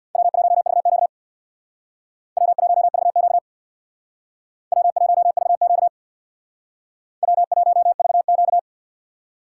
Morse code, heard as a keyed tone, about 59 wpm, 700 Hz. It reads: W1VQ W1VQ W1VQ W1VQ
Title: Speed Racing - USA Call Signs 50wpm